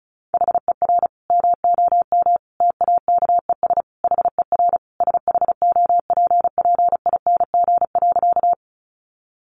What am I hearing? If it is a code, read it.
Morse code, heard as a keyed tone, about 34 wpm, 700 Hz. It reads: HER MOM TAKES HER SHOPPING.
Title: Additional Easy Phrases 35wpm